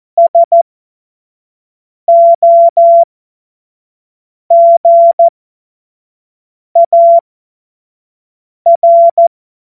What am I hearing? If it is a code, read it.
Morse code, heard as a keyed tone, about 14 wpm, spaced as for 5 wpm, 680 Hz. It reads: SOGAR
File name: Morsekurs / Witze / Witz 04. Oktober 2022 25 F70